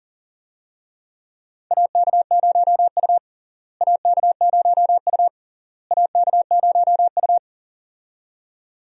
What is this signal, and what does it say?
Morse code, heard as a keyed tone, about 37 wpm, 700 Hz. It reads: AK0U AK0U AK0U
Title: USA Call Signs - 4 Characters - Repeated 3x - 40wpm